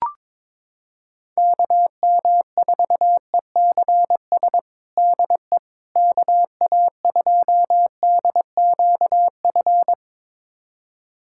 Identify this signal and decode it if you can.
Morse code, heard as a keyed tone, about 22 wpm, 700 Hz. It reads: KM4ECS DE KA2DQF
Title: USA Callsign DE Callsign - 6 Characters - 22wpm